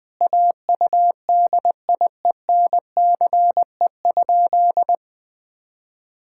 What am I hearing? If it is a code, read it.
Morse code, heard as a keyed tone, about 20 wpm, 700 Hz. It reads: AUDIENCE?
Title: Sentences from Top 1000 Words - Review Part 5 - 20wpm